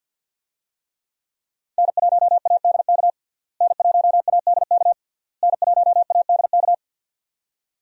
Morse code, heard as a keyed tone, about 52 wpm, 700 Hz. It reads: N1ADK N1ADK N1ADK
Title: USA Call Signs - 5 Characters - Repeated 3x - 50wpm